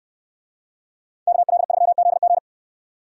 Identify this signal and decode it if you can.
Morse code, heard as a keyed tone, about 68 wpm, 700 Hz. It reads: KL3ZG